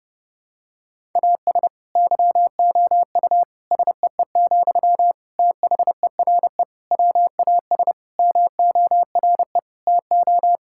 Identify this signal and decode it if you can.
Morse code, heard as a keyed tone, about 30 wpm, 700 Hz. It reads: AS YOU SEE, THERE WAS MORE TO
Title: Sentences from Top 1000 Words - Review Part 6 - 30wpm